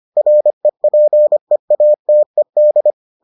Morse code, heard as a keyed tone, about 25 wpm, 600 Hz. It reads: REPEATED